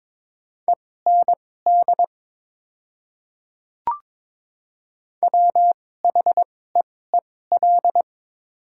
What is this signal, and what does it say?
Morse code, heard as a keyed tone, about 22 wpm, spaced as for 16 wpm, 690 Hz. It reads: END WHEEL
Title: 2x Character Spacing - Top 500 Words - 22wpm